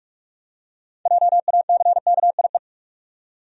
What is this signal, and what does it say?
Morse code, heard as a keyed tone, about 43 wpm, 700 Hz. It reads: JACKIE
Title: USA Common Names - 45wpm